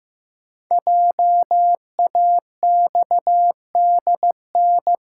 Morse code, heard as a keyed tone, about 15 wpm, 700 Hz. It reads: JAXDN
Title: Code Groups - 5 Letters - 15wpm